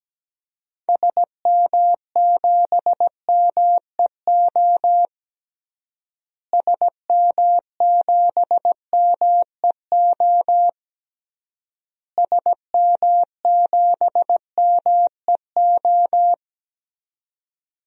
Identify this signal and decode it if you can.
Morse code, heard as a keyed tone, about 17 wpm, 700 Hz. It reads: SM7MEO SM7MEO SM7MEO
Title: DXCC-100 Call Signs - 6 Characters - Repeated 3x - 17wpm